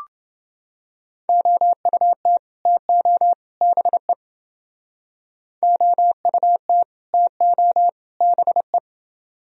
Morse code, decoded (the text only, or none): OUT TO BE OUT TO BE